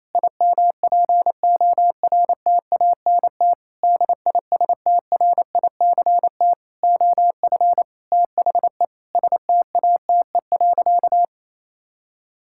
Morse code, heard as a keyed tone, about 28 wpm, 700 Hz. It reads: IMPORTANT DISTRICT OF THE STATE.
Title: Sentences from Top 1600 Words 28wpm